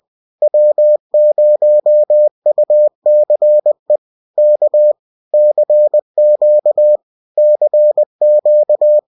Morse code, heard as a keyed tone, about 20 wpm, 600 Hz. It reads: W0UCE K CQ CQ